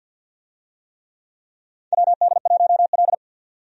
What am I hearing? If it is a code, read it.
Morse code, heard as a keyed tone, about 52 wpm, 700 Hz. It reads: WD1L
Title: USA Call Signs - 4 Characters - 50wpm